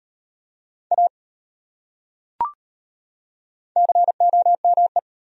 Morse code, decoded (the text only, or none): A COME